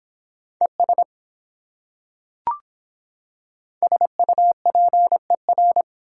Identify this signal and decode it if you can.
Morse code, heard as a keyed tone, about 24 wpm, 700 Hz. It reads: ES SUPER